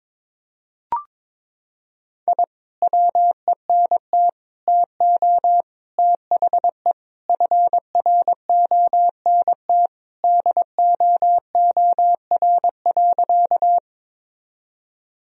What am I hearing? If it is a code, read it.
Morse code, heard as a keyed tone, about 22 wpm, 700 Hz. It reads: I WENT TO THE FRONT DOOR.
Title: Sentences from Top 1000 Words - Review Part 2 - 22wpm